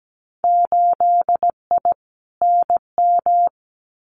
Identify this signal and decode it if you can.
Morse code, heard as a keyed tone, about 17 wpm, 700 Hz. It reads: 8I NM
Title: Winter Field Day Exchanges Encore 17wpm